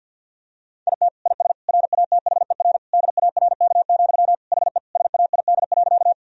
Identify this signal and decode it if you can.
Morse code, heard as a keyed tone, about 50 wpm, 700 Hz. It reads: IT IS RATHER DARK, HE SAID.
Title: Sentences from Top 1000 Words - Review Part 4 - 50wpm